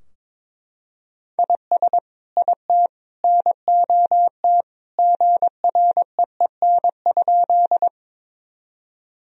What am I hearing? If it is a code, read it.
Morse code, heard as a keyed tone, about 22 wpm, 700 Hz. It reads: IS IT NOT GREEN?
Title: Sentences from Top 1000 Words - Review Part 2 - 22wpm